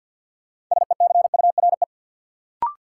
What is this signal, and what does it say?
Morse code, heard as a keyed tone, about 53 wpm, 700 Hz. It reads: SECURE
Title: Sentences from Top 1200 Words - New Word Drill - 50wpm